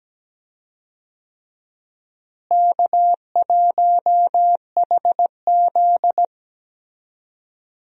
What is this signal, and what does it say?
Morse code, heard as a keyed tone, about 17 wpm, 700 Hz. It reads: K1HZ
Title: USA Call Signs - 4 Characters - Repeated 2x - 17wpm